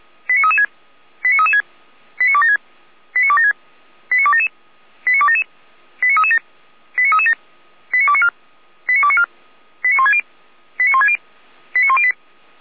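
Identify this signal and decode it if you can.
Five-tone selective call (ZVEI), heard as quick runs of five tones, tones of about 70 ms. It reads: 89336 89336 89276 89276 89277 89277 89337 89337 89294 89294 89160 89160 89108
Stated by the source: ZVEI five-tone selective call received by SDR.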